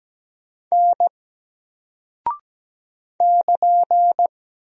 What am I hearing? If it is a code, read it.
Morse code, heard as a keyed tone, about 17 wpm, 700 Hz. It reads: N <KN>
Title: Single Character (Letter:Number:Punctuation:Prosign) 17wpm